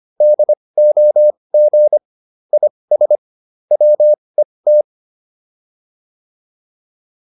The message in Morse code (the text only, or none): DOG IS WET